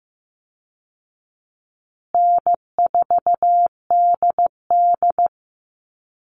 Morse code, heard as a keyed tone, about 15 wpm, 700 Hz. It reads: N4DD
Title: Call Signs Repeated 3x - 15wpm